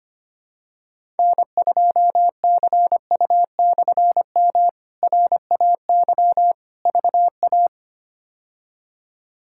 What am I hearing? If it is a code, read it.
Morse code, heard as a keyed tone, about 25 wpm, 700 Hz. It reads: N2CU/M RAY VA
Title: K1USN SST QSO Practice - 25wpm